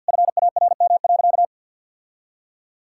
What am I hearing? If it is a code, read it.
Morse code, heard as a keyed tone, about 51 wpm, 700 Hz. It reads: FARM.
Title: Sentences from Top 1000 Words - Review Part 6 - 50wpm